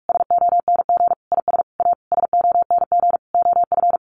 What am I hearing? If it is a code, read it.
Morse code, heard as a keyed tone, about 44 wpm, 700 Hz. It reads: SONG IS A SONG OF
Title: Sentences from Top 1500 Words 45wpm